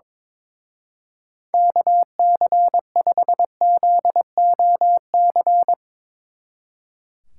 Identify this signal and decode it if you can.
Morse code, heard as a keyed tone, about 24 wpm, 700 Hz. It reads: KC5ZOC
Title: Speed Racing - USA Call Signs 22wpm